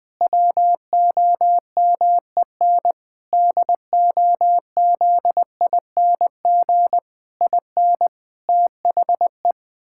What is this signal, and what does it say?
Morse code, heard as a keyed tone, about 20 wpm, 700 Hz. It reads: WOMEN DOZING IN THE
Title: Pangram Challenge v2 - 20wpm